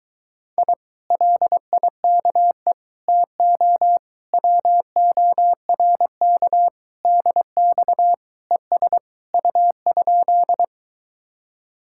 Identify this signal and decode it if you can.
Morse code, heard as a keyed tone, about 23 wpm, 700 Hz. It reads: I LIKE TO WORK DX ES U?